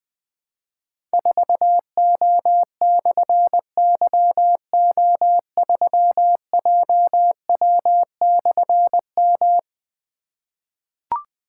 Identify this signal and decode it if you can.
Morse code, heard as a keyed tone, about 20 wpm, 700 Hz. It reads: 4O/YO3JW/M
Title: Difficult Call Signs 20wpm